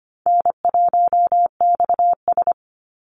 Morse code, heard as a keyed tone, about 25 wpm, 700 Hz. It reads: N1XS
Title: Easy Call Signs Repeated 3x - 25wpm